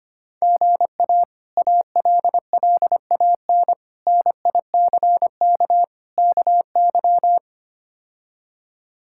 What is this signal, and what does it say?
Morse code, heard as a keyed tone, about 25 wpm, 700 Hz. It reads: GA ALLAN NICK KY
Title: K1USN SST QSO Practice - 25wpm